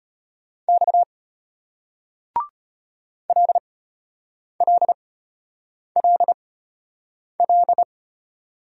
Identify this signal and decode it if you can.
Morse code, heard as a keyed tone, about 35 wpm, spaced as for 8 wpm, 700 Hz. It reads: X LLLL